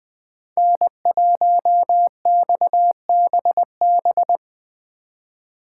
Morse code, heard as a keyed tone, about 20 wpm, 700 Hz. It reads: N1XBB
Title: USA Call Signs - 5 Characters - 3x Character Spacing - 20wpm